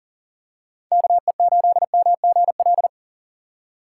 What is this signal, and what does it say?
Morse code, heard as a keyed tone, about 40 wpm, 700 Hz. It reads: KE8MGL
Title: USA Call Signs - 3x Character Spacing - 40wpm